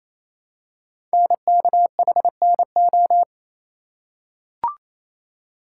Morse code, heard as a keyed tone, about 27 wpm, 700 Hz. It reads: NKHNO
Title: Code Groups - 5 Letters - 28wpm